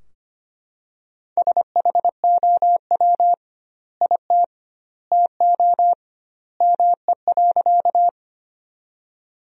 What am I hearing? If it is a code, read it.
Morse code, heard as a keyed tone, about 25 wpm, 700 Hz. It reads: SHOW IT TO ME.
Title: 2x Word-Spacing - Sentences from Top 200 Words - 25wpm